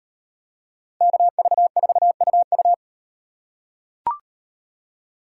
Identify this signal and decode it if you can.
Morse code, heard as a keyed tone, about 36 wpm, 700 Hz. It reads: KV4UU